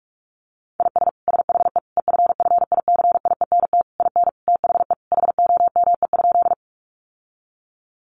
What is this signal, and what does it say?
Morse code, heard as a keyed tone, about 44 wpm, 700 Hz. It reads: IS SHE EFFICIENT IN THE HOME?
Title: Sentences from Top 1300 Words 45wpm